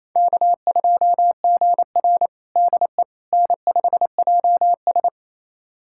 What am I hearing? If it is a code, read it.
Morse code, heard as a keyed tone, about 28 wpm, 700 Hz. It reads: K2GR DE N5JS
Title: USA Callsign DE Callsign - 4 Characters - 28wpm